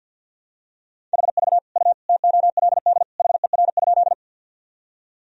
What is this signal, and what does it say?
Morse code, heard as a keyed tone, about 51 wpm, 700 Hz. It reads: HV U TOLD HER?
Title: Sentences from Top 400 Words and Abbreviations 50wpm